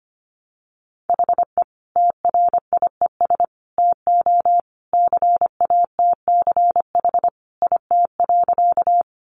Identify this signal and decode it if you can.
Morse code, heard as a keyed tone, about 25 wpm, 700 Hz. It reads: HE TRIES TO CATCH IT.
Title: Additional Easy Phrases 25wpm